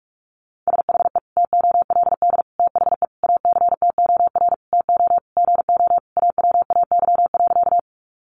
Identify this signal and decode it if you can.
Morse code, heard as a keyed tone, about 44 wpm, 700 Hz. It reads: SHE TOLD THE ACTOR TO GO AWAY.
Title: Sentences from Top 1500 Words 45wpm